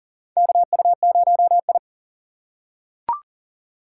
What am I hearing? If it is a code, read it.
Morse code, heard as a keyed tone, about 40 wpm, 700 Hz. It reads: KU0I
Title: USA Call Signs - 4 Characters - 2x Character Spacing - 40wpm